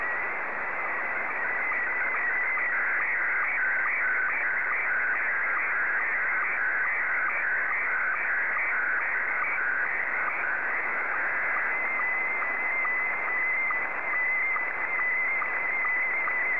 SSTV (slow-scan television), Scottie 1.